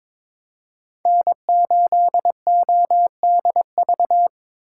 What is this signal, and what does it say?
Morse code, heard as a keyed tone, about 22 wpm, 700 Hz. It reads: N8ODV
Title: USA Call Signs - 5 Characters - 2x Character Spacing - 22wpm